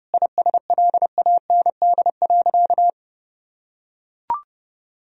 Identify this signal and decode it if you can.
Morse code, heard as a keyed tone, about 32 wpm, 700 Hz. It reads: ISLAND.
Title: Sentences from Top 1000 Words - Review Part 3 - 30wpm